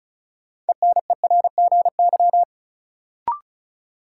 Morse code, heard as a keyed tone, about 34 wpm, 700 Hz. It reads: ENERGY
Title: Sentences from Top 900 Words - New Word Drill - 35wpm